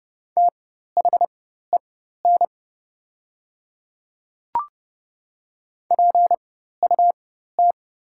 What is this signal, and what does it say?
Morse code, heard as a keyed tone, about 29 wpm, spaced as for 14 wpm, 700 Hz. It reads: THEN PUT
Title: 4x Character Spacing - Top 300 Words - 30wpm